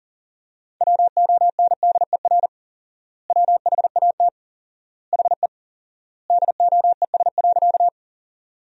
Morse code, heard as a keyed tone, about 40 wpm, 700 Hz. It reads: WONDER WHAT HE DOES.
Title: 4x Word-Spacing - Sentences from Top 500 Words - 40wpm